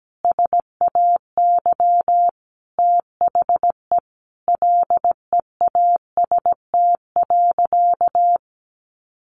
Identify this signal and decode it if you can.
Morse code, heard as a keyed tone, about 17 wpm, 700 Hz. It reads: SAY THE LEAST.
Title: Sentences from Top 1200 Words 17wpm